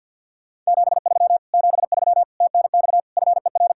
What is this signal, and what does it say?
Morse code, heard as a keyed tone, about 64 wpm, 700 Hz. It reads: 73 73 TNX FER